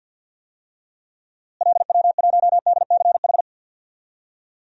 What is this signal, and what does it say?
Morse code, heard as a keyed tone, about 52 wpm, 700 Hz. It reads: LW1DCH
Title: DXCC-100 Call Signs - 6 Characters - Repeated 2x - 50wpm